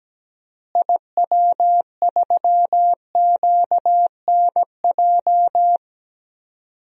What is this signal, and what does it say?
Morse code, heard as a keyed tone, about 17 wpm, 700 Hz. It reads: IW3QNJ
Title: DXCC-100 Call Signs - 6 Characters - Repeated 3x - 17wpm